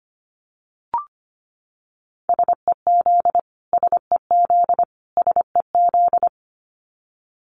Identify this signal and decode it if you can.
Morse code, heard as a keyed tone, about 25 wpm, 700 Hz. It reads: SEZ SEZ SEZ
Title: Abbreviations Repeated 3x - 25wpm